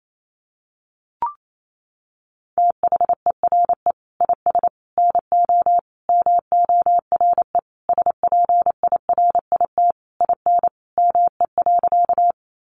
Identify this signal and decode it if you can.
Morse code, heard as a keyed tone, about 28 wpm, 700 Hz. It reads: THERE IS NO MORE SPIRIT IN ME.
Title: Sentences from Top 1400 Words 28wpm